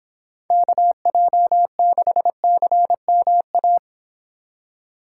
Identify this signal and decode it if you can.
Morse code, heard as a keyed tone, about 26 wpm, 700 Hz. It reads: KJ6CMA